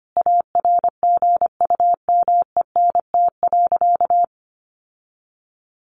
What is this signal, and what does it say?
Morse code, heard as a keyed tone, about 25 wpm, 700 Hz. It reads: ARGUMENT.
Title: Sentences from Top 1400 Words 25wpm